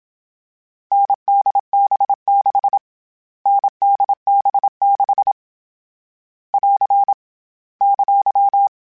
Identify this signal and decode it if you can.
Morse code, heard as a keyed tone, about 27 wpm, 800 Hz. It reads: NDB6 NDB6 <AR> !
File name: Short Runs / Short Runs-mp3s / NDB NDB6 @25